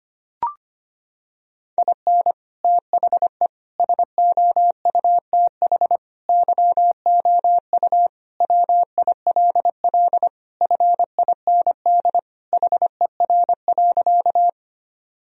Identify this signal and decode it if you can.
Morse code, heard as a keyed tone, about 25 wpm, 700 Hz. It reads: IN THE SOUTH YOU WILL FIND HER.
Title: Sentences from Top 1000 Words - Review Part 2 - 25wpm